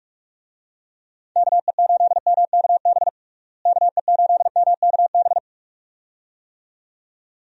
Morse code, heard as a keyed tone, about 42 wpm, 700 Hz. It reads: KE8MKB KE8MKB
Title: USA Call Signs - 6 Characters - Repeated 2x - 45wpm